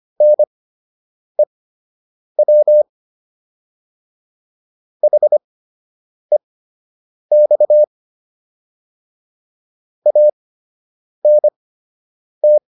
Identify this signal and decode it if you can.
Morse code, heard as a keyed tone, about 25 wpm, spaced as for 8 wpm, 600 Hz. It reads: NEW HEX ANT